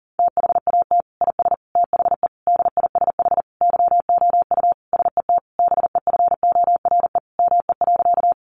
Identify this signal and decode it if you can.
Morse code, heard as a keyed tone, about 40 wpm, 700 Hz. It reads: THAT IS THE DISH YOU SET BEFORE ME.
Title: Sentences from Top 1300 Words 40wpm